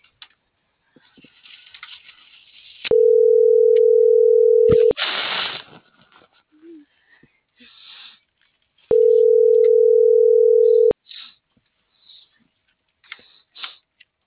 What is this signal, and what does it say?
Ringback tone (North America), 440 + 480 Hz.